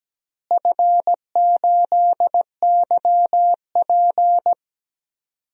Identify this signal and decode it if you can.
Morse code, heard as a keyed tone, about 17 wpm, 700 Hz. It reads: F8YP
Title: DXCC-100 Call Signs - 4 Characters - Repeated 2x - 17wpm